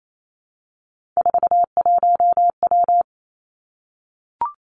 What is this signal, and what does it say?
Morse code, heard as a keyed tone, about 28 wpm, 700 Hz. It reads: 41W
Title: Three Challenging Letter-Number Combinations 28wpm